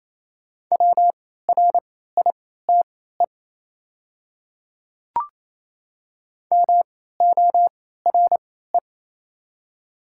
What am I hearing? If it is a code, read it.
Morse code, heard as a keyed tone, about 28 wpm, spaced as for 16 wpm, 700 Hz. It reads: WRITE MORE
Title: 3x Character Spacing - Top 100 Words - 28wpm